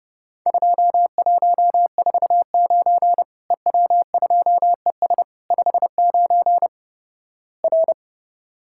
Morse code, heard as a keyed tone, about 32 wpm, 700 Hz. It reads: 2149 EW2ES 59 R
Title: Worked All Europe DX Contest - QTC Exchange - 30wpm